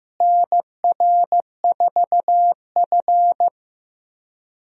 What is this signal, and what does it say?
Morse code, heard as a keyed tone, about 15 wpm, 700 Hz. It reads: NR4F
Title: USA Call Signs - 4 Characters - Repeated 2x - 15wpm